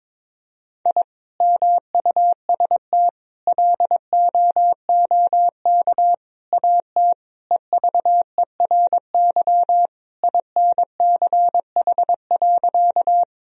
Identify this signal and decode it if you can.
Morse code, heard as a keyed tone, about 22 wpm, 700 Hz. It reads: I MUST LOOK AT EVERY INCH.
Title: Sentences from Top 1000 Words - Review Part 3 - 22wpm